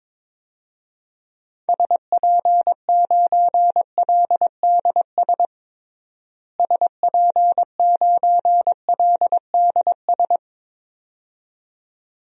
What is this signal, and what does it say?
Morse code, heard as a keyed tone, about 22 wpm, 700 Hz. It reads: SP9LDS SP9LDS
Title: DXCC-100 Call Signs - 6 Characters - Repeated 2x - 22wpm